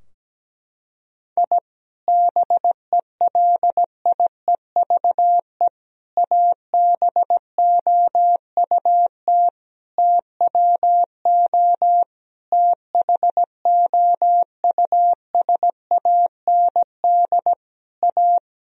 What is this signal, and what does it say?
Morse code, heard as a keyed tone, about 17 wpm, 700 Hz. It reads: I BELIEVE ABOUT TWO THOUSAND A
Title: Sentences from Top 1000 Words - Review Part 3 - 17wpm